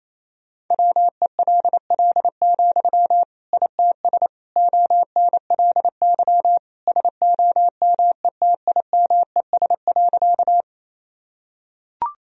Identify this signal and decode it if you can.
Morse code, heard as a keyed tone, about 28 wpm, 700 Hz. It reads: WELL, ITS ONLY SOMETIMES.
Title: Sentences from Top 1000 Words - Review Part 3 - 28wpm